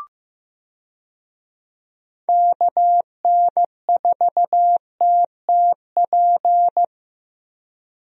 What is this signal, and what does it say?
Morse code, heard as a keyed tone, about 15 wpm, 700 Hz. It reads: KN4TTP
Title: USA Call Signs - 6 Characters - Repeated 2x - 15wpm